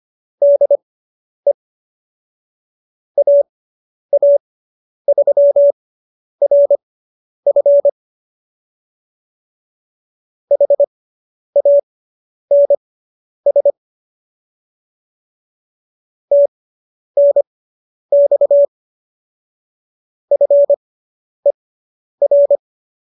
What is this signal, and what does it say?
Morse code, heard as a keyed tone, about 25 wpm, spaced as for 10 wpm, 570 Hz. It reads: DE AA3RF HANS TNX FER